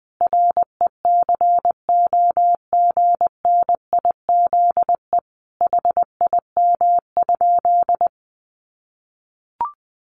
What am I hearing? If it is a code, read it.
Morse code, heard as a keyed tone, about 20 wpm, 700 Hz. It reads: RECOGNIZE HIM?
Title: Sentences from Top 1400 Words 20wpm